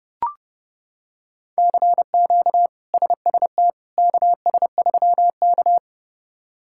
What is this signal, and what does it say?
Morse code, heard as a keyed tone, about 30 wpm, 700 Hz. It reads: CQ SST KS3K
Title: K1USN SST QSO Practice - 30wpm